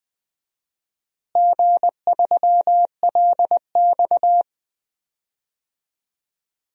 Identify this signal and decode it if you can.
Morse code, heard as a keyed tone, about 20 wpm, 700 Hz. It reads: G3LX
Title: DXCC-100 Call Signs - 4 Characters - Repeated 3x - 20wpm